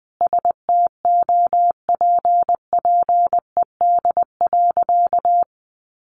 Morse code, heard as a keyed tone, about 20 wpm, 700 Hz. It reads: STOPPED.
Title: A Sound Of Thunder 20wpm